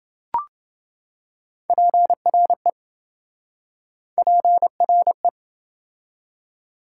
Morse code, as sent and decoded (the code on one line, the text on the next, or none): .--. .-. / .--. .-. .
PRE PRE